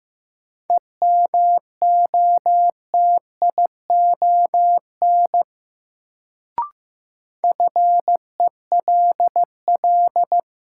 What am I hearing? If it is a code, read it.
Morse code, heard as a keyed tone, about 15 wpm, 700 Hz. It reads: EMOTION FELL
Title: Sentences from Top 1600 Words - New Word Drill Alt - 15wpm